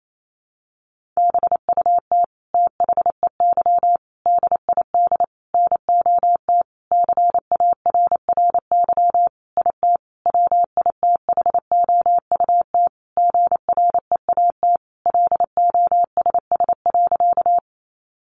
Morse code, heard as a keyed tone, about 28 wpm, 700 Hz. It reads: BUT THEY DID NOT CARRY IT WITHOUT GREAT LOSS.
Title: Sentences from Top 1400 Words 28wpm